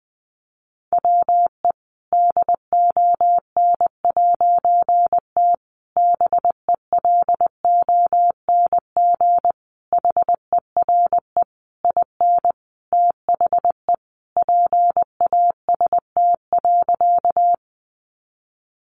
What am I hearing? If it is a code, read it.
Morse code, heard as a keyed tone, about 20 wpm, 700 Hz. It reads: WE DON'T BELONG HERE IN THE PAST.
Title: A Sound Of Thunder 20wpm